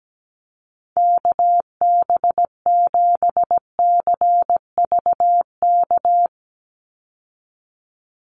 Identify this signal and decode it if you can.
Morse code, heard as a keyed tone, about 17 wpm, 700 Hz. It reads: KB7CVK
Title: USA Call Signs 17wpm